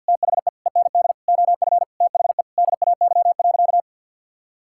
Morse code, heard as a keyed tone, about 50 wpm, 700 Hz. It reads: THE END OF THE DAY.
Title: Sentences from Top 1000 Words - Review Part 6 - 50wpm